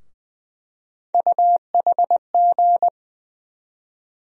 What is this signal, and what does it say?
Morse code, heard as a keyed tone, about 20 wpm, 700 Hz. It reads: UHG
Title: USA Call Sign Postfixes - 3 Letters 20wpm